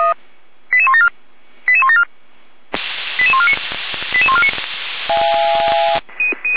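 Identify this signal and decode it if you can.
Five-tone selective call (ZVEI), heard as quick runs of five tones, tones of about 70 ms. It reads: 88174 88174 88149 88149
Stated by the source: ZVEI five-tone selective call received on SDR, Feuerwehr Orlamünde fire brigade.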